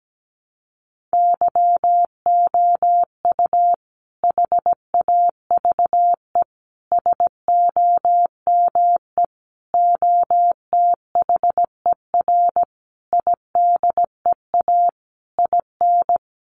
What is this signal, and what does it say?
Morse code, heard as a keyed tone, about 17 wpm, 700 Hz. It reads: YOU HAVE SOME OTHER IDEA IN
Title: Sentences from Top 1300 Words 17wpm